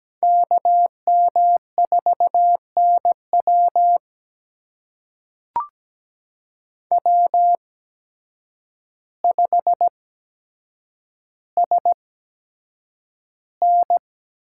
Morse code, heard as a keyed tone, about 17 wpm, spaced as for 8 wpm, 700 Hz. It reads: KM4NW W5SN
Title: USA Call Signs - 5 Characters - 8x Character Spacing - 17wpm